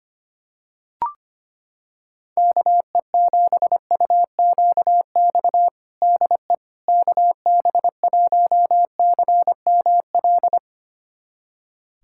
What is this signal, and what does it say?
Morse code, heard as a keyed tone, about 25 wpm, 700 Hz. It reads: KE7UQX DE KB1CML
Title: USA Callsign DE Callsign 25wpm